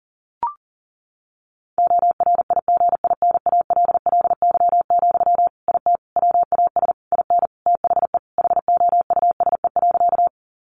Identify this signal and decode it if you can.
Morse code, heard as a keyed tone, about 40 wpm, 700 Hz. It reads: ORIGINALLY, IT WAS IN THE HOUSE.
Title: Sentences from Top 1500 Words 40wpm